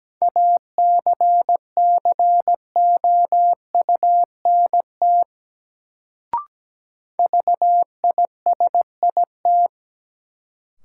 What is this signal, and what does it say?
Morse code, heard as a keyed tone, about 17 wpm, 700 Hz. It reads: ACCOUNT VISIT
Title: Sentences from Top 900 Words - New Word Drill - 17wpm